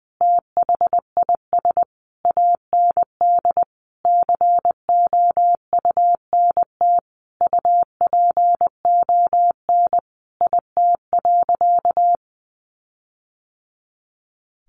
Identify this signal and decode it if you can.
Morse code, heard as a keyed tone, about 20 wpm, 700 Hz. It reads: THIS AND COUNT UPON IT.
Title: Sentences from Top 1500 Words 20wpm